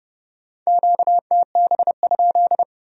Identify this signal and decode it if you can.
Morse code, heard as a keyed tone, about 37 wpm, 700 Hz. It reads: QTB?
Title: Speed-Racing Q-Codes - 30wpm